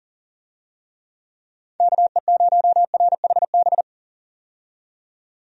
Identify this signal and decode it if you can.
Morse code, heard as a keyed tone, about 40 wpm, 700 Hz. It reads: KE0RHB
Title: USA Call Signs - 6 Characters - 40wpm